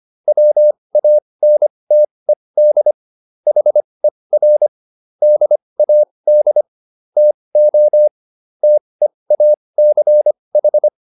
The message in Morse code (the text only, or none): WANTED HER DAD TO TEACH